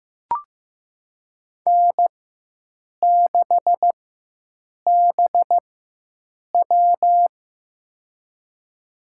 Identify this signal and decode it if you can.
Morse code, heard as a keyed tone, about 15 wpm, spaced as for 7 wpm, 700 Hz. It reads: N6BW
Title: USA Call Signs - 4 Characters - 4x Character Spacing - 15wpm